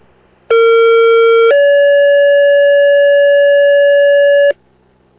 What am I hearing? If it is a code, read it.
Two-tone page, heard as a one-second tone, then a three-second tone, about 460 then 570 Hz. It is heard from 0.5 to 4.5 s.